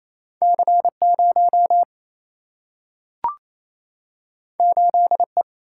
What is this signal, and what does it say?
Morse code, heard as a keyed tone, about 28 wpm, 700 Hz. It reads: C0 8E
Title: Single-Letter Single-Number 28wpm